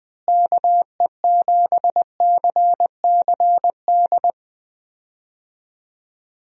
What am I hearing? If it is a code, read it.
Morse code, heard as a keyed tone, about 20 wpm, 700 Hz. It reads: KE7CCD
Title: USA Call Signs - 6 Characters - Repeated 2x - 20wpm